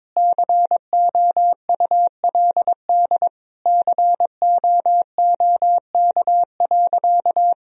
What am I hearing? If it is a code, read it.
Morse code, heard as a keyed tone, about 22 wpm, 700 Hz. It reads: COULD COOK.
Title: Sentences from Top 1000 Words - Review Part 4 - 22wpm